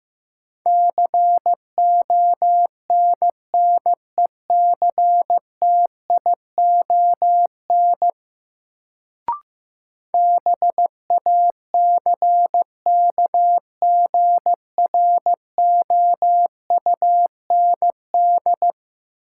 Morse code, heard as a keyed tone, about 15 wpm, 700 Hz. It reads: CONNECTION BACKGROUND
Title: Sentences from Top 1300 Words - New Word Drill - 15wpm